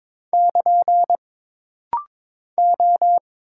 Morse code, heard as a keyed tone, about 22 wpm, 700 Hz. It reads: <KN> O
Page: This rapid-fire timing is geared to push your ICR (Instant Character Recognition) skills.